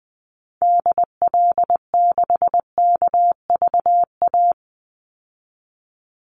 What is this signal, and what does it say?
Morse code, heard as a keyed tone, about 20 wpm, 700 Hz. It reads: DL6KVA
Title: Call Signs Repeated 2x - 20wpm